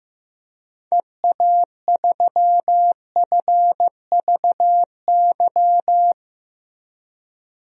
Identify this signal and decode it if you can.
Morse code, heard as a keyed tone, about 15 wpm, 700 Hz. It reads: EA3FVY
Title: DXCC-100 Call Signs - 6 Characters - Repeated 3x - 15wpm